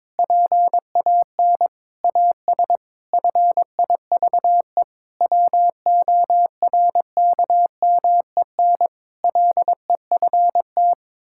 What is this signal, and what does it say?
Morse code, heard as a keyed tone, about 22 wpm, 700 Hz. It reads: PAN AS FIVE WORKMEN LEFT
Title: Pangram Challenge v2 - 22wpm